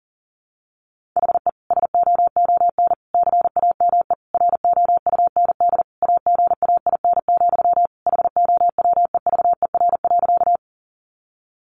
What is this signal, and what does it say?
Morse code, heard as a keyed tone, about 37 wpm, 700 Hz. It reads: HE SOON CAME ROUND AGAIN, HOWEVER.
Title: Sentences from Top 600 Words 40wpm